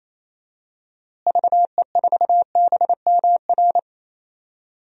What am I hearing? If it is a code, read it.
Morse code, heard as a keyed tone, about 28 wpm, 700 Hz. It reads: VE4BMR
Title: DXCC-100 Call Signs - 6 Characters - Repeated 3x - 28wpm